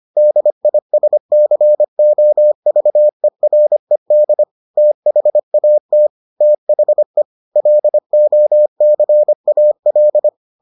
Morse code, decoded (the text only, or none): DISCOVERED THAT THE LOCAL